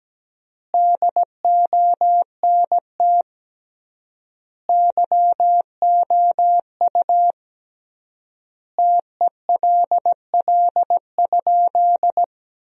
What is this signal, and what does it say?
Morse code, heard as a keyed tone, about 17 wpm, 700 Hz. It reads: DONT YOU TELL?
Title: 3x Word-Spacing - Sentences from Top 300 Words - 17wpm